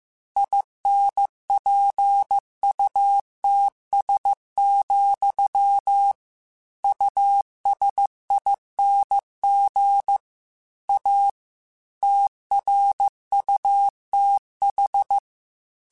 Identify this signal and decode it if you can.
Morse code, heard as a keyed tone, about 15 wpm, 800 Hz. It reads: INPUTS, USING A TRUTH
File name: Technical Subjects / Technical Subjects-mp3s / Gates Intro @13